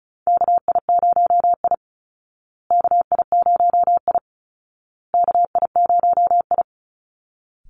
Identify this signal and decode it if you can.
Morse code, heard as a keyed tone, about 34 wpm, 700 Hz. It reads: KI0I KI0I KI0I
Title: Call Signs Repeated 2x - 35wpm